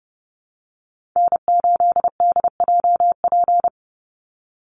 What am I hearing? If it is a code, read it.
Morse code, heard as a keyed tone, about 30 wpm, 700 Hz. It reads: N8DJP
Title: USA Call Signs 30wpm